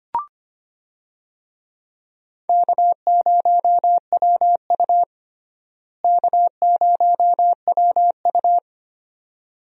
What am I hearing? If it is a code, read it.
Morse code, heard as a keyed tone, about 25 wpm, 700 Hz. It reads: K0WU K0WU
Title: USA Call Signs - 4 Characters - Repeated 2x - 25wpm